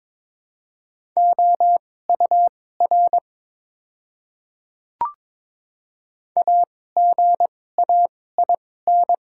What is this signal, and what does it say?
Morse code, heard as a keyed tone, about 22 wpm, spaced as for 16 wpm, 700 Hz. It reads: OUR AGAIN
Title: 2x Character Spacing - Top 200 Words - 22wpm